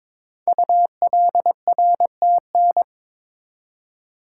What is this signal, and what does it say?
Morse code, heard as a keyed tone, about 22 wpm, 700 Hz. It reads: ULRTN
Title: Code Groups - 5 Letters - 22wpm